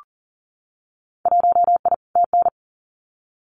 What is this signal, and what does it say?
Morse code, heard as a keyed tone, about 38 wpm, 700 Hz. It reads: 1I TN
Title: Winter Field Day Exchanges Encore 40wpm